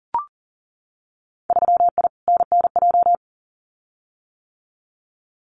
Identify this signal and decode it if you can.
Morse code, heard as a keyed tone, about 41 wpm, 700 Hz. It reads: 3I NNJ
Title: Winter Field Day Exchanges 40wpm